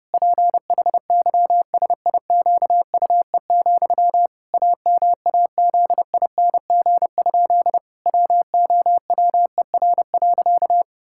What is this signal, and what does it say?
Morse code, heard as a keyed tone, about 30 wpm, 700 Hz. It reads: PHYSIQUE, AMAZING? WOWER.
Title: Pangram Challenge v2 - 30wpm